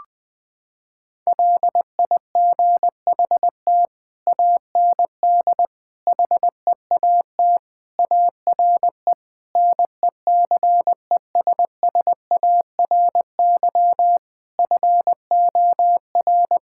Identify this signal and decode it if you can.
Morse code, heard as a keyed tone, about 21 wpm, 700 Hz. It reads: LIGHT AND HEAT ARE NECESSARY FOR
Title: Sentences from Top 1000 Words - Review Part 5 - 20wpm